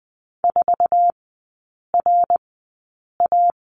Morse code, heard as a keyed tone, about 20 wpm, spaced as for 8 wpm, 700 Hz. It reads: 4RA
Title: 053 Lesson 10e - Letter r - Three Characters 20wpm